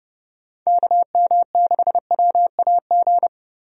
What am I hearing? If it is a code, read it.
Morse code, heard as a keyed tone, about 30 wpm, 700 Hz. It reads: KM6WAG
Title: USA Call Signs - 6 Characters - Repeated 3x - 30wpm